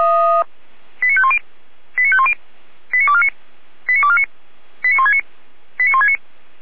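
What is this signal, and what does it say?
Five-tone selective call (ZVEI), heard as quick runs of five tones, tones of about 70 ms. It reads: 89510 89510 89249 89249 89169 89169
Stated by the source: ZVEI five-tone call 89167 received by SDR, logged by Feuerwehr Orlamuende (f85360).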